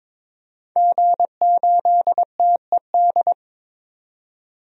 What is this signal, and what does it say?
Morse code, heard as a keyed tone, about 22 wpm, 700 Hz. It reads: G8TED
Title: DXCC-100 Call Signs - 5 Characters - 22wpm